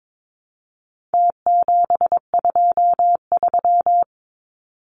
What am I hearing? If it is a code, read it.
Morse code, heard as a keyed tone, about 22 wpm, 700 Hz. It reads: T723
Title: Numbers - Contest Style - 4-Digit 22wpm